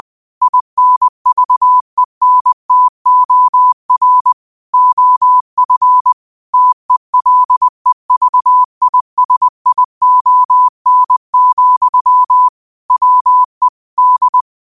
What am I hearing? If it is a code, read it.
Morse code, heard as a keyed tone, about 20 wpm, 1000 Hz. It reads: INVENTOR OF TELEVISION, WE D
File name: 20wpm